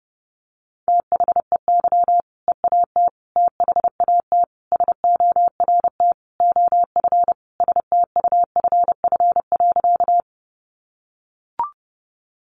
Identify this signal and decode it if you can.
Morse code, heard as a keyed tone, about 30 wpm, 700 Hz. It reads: THEY EAT THAT SORT OF STUFF.
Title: Sentences from Top 1400 Words 30wpm